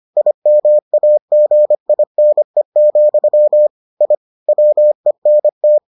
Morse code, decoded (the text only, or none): IMAGINE, I WENT